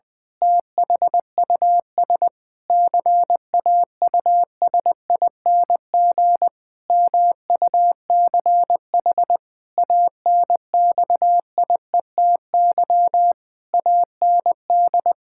Morse code, heard as a keyed tone, about 20 wpm, 700 Hz. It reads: THUS CAUSING MUCH ANXIETY AND
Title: Pangram Challenge v2 - 20wpm